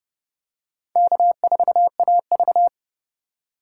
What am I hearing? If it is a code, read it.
Morse code, heard as a keyed tone, about 29 wpm, 700 Hz. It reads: K4AV
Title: USA Call Signs - 4 Characters - 4x Character Spacing - 30wpm